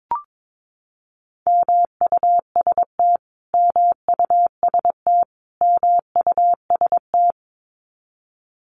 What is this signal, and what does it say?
Morse code, heard as a keyed tone, about 22 wpm, 700 Hz. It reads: MUST MUST MUST
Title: Top 300 Words Repeated 22wpm